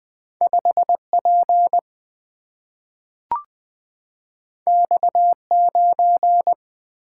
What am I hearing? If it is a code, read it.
Morse code, heard as a keyed tone, about 20 wpm, 700 Hz. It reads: 5 P X 9
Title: Single-Letter Single-Number 20wpm